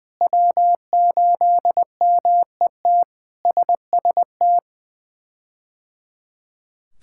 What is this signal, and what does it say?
Morse code, heard as a keyed tone, about 20 wpm, 700 Hz. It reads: W8MET SST
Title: K1USN SST QSO Practice - 20wpm